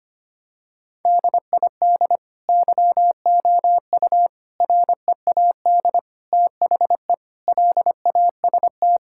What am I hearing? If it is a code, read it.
Morse code, heard as a keyed tone, about 25 wpm, 700 Hz. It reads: DID YOU READ THE LAST
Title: Sentences from Top 1000 Words - Review Part 5 - 25wpm